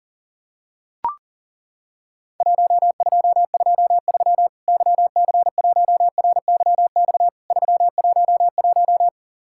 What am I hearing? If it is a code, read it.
Morse code, heard as a keyed tone, about 40 wpm, 700 Hz. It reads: 1223 YC1RYX 311
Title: Worked All Europe DX Contest - QTC Exchange - 40wpm